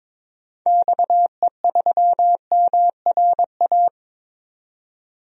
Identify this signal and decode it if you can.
Morse code, heard as a keyed tone, about 22 wpm, 700 Hz. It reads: XE3MRA
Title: DXCC-100 Call Signs - 6 Characters - 22wpm